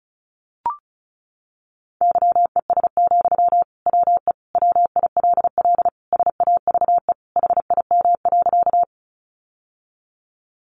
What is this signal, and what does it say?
Morse code, heard as a keyed tone, about 34 wpm, 700 Hz. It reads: YES, WE WILL SAVE HIM.
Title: Sentences from Top 1100 Words 35wpm